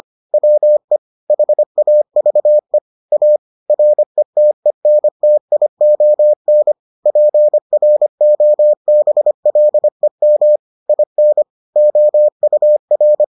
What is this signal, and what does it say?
Morse code, heard as a keyed tone, about 25 wpm, 600 Hz. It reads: WE HAVE A RETENTION PROBLEM IN OUR